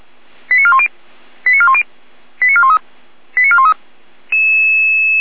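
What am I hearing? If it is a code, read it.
Five-tone selective call (ZVEI), heard as quick runs of five tones, tones of about 70 ms. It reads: 89410 89410 89413 89413
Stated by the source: ZVEI five-tone selective call to address 88177, received on an SDR.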